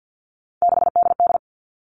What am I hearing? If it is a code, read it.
Morse code, heard as a keyed tone, about 50 wpm, 700 Hz. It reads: 6DD
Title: Three Challenging Letter-Number Combinations 50wpm